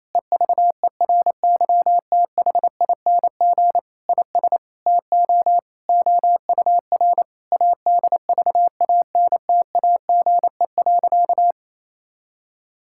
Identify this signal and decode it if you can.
Morse code, heard as a keyed tone, about 27 wpm, 700 Hz. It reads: EVERYTHING IS TO OUR ADVANTAGE.
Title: Sentences from Top 1000 Words - Review Part 6 - 28wpm